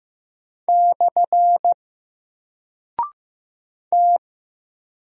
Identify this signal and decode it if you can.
Morse code, heard as a keyed tone, about 15 wpm, 700 Hz. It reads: / T
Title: Single Character (Letter:Number:Punctuation:Prosign) 15wpm